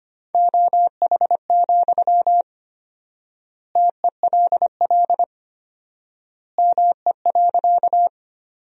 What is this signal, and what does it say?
Morse code, heard as a keyed tone, about 25 wpm, 700 Hz. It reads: OH, TELL ME.
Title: 4x Word-Spacing - Sentences from Top 500 Words - 25wpm